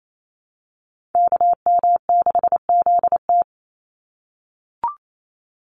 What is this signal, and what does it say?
Morse code, heard as a keyed tone, about 28 wpm, 700 Hz. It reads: KM6ZT
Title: Call Signs - 28wpm